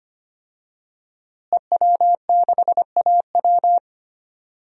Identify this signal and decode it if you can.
Morse code, heard as a keyed tone, about 25 wpm, 700 Hz. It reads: EW6AW
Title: DXCC-100 Call Signs - 5 Characters - Repeated 2x - 25wpm